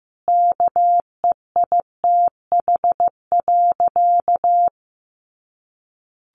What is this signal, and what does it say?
Morse code, heard as a keyed tone, about 15 wpm, 700 Hz. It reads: KEITH.
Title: A Sound Of Thunder 15wpm